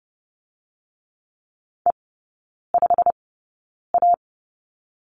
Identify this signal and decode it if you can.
Morse code, heard as a keyed tone, about 30 wpm, spaced as for 9 wpm, 700 Hz. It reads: E5A
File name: Lesson 16e - Letter c - Three Characters 30wpm